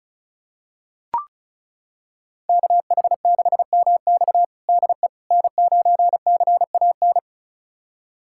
Morse code, heard as a keyed tone, about 36 wpm, 700 Hz. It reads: KH6MX DE N9CAN
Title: USA Callsign DE Callsign - 5 Characters - 35wpm